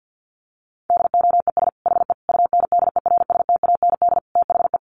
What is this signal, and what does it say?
Morse code, heard as a keyed tone, about 50 wpm, 700 Hz. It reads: DOES HE UNDERSTAND THE
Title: Sentences from Top 1100 Words 50wpm